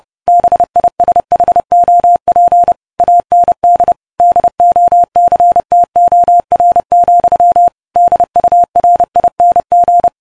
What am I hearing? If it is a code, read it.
Morse code, heard as a keyed tone, about 30 wpm, 700 Hz. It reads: BISHOP AND DOCTOR, DURING